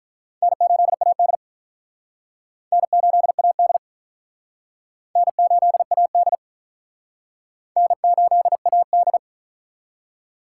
Morse code, heard as a keyed tone, about 45 wpm, 700 Hz. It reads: N8AD N8AD N8AD N8AD